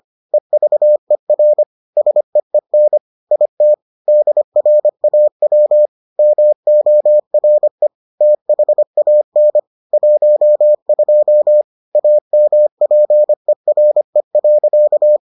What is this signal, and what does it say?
Morse code, heard as a keyed tone, about 25 wpm, 600 Hz. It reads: EVER SEEN IT DRAW MORE THAN 12 AMPERE.